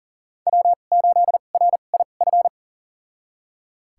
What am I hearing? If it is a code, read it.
Morse code, heard as a keyed tone, about 40 wpm, spaced as for 29 wpm, 700 Hz. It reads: W8RIF
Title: USA Call Signs - 2x Character Spacing - 40wpm